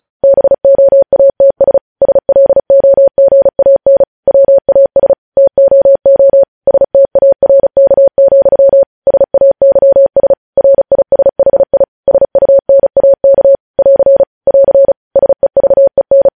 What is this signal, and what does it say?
Morse code, heard as a keyed tone, about 36 wpm, 550 Hz. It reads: BOATS SLOGAN WAS TOO STARK, SAYS RISHI SUNAK <AR> <AR> SEVEN
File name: podcast / News headlines-1x-35x35